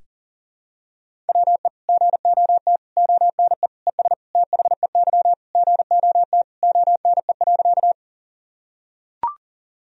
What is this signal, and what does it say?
Morse code, heard as a keyed tone, about 40 wpm, 700 Hz. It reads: WE GOT ONE ES THEY GOT ONE.
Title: Sentences from Top 300 Words and Abbreviations 40wpm